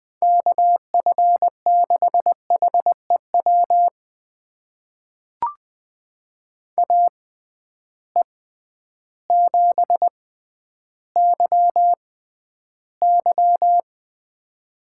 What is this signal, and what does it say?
Morse code, heard as a keyed tone, about 20 wpm, spaced as for 11 wpm, 700 Hz. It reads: KF6HEW AE7YY
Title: USA Call Signs - 6x Character Spacing - 20wpm